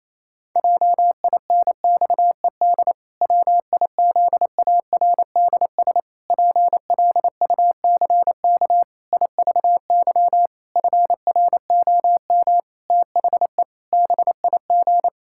Morse code, heard as a keyed tone, about 27 wpm, 700 Hz. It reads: JINXED WIZARDS PLUCK IVY FROM THE BIG
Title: Pangram Challenge v2 - 28wpm